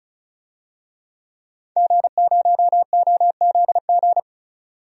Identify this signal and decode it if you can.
Morse code, heard as a keyed tone, about 36 wpm, 700 Hz. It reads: G0OZG
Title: DXCC-100 Call Signs - 5 Characters - 35wpm